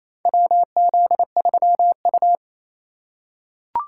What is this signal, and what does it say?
Morse code, heard as a keyed tone, about 28 wpm, 700 Hz. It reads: WZ3U
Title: USA Call Signs - 4 Characters - 2x Character Spacing - 28wpm